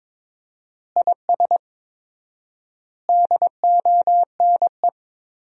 Morse code, heard as a keyed tone, about 22 wpm, 700 Hz. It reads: IS DONE
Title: 4x Word-Spacing - Sentences from Top 500 Words - 22wpm